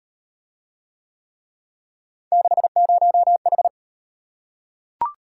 Morse code, heard as a keyed tone, about 33 wpm, 700 Hz. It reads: 60H